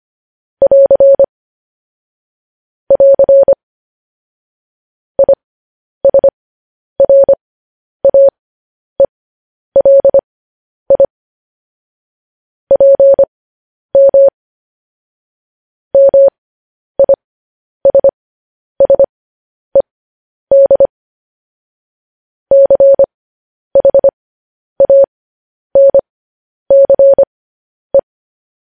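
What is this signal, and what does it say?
Morse code, heard as a keyed tone, about 25 wpm, spaced as for 10 wpm, 550 Hz. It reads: <AR> <AR> ISRAELI PM MISSED CHANCE
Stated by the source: Morse code news headlines.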